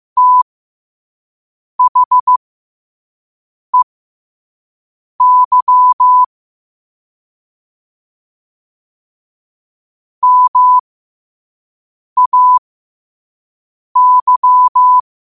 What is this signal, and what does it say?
Morse code, heard as a keyed tone, about 15 wpm, spaced as for 5 wpm, 1000 Hz. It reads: THEY MAY
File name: qotdfiles / 5wpm